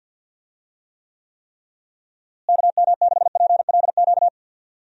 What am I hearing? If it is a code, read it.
Morse code, heard as a keyed tone, about 46 wpm, 700 Hz. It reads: KM6PLQ
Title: USA Call Signs - 6 Characters - Repeated 3x - 50wpm